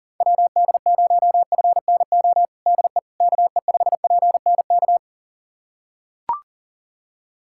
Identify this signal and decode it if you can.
Morse code, heard as a keyed tone, about 40 wpm, 700 Hz. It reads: WD0FNO DE KE5PNK
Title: USA Callsign DE Callsign - 6 Characters - 40wpm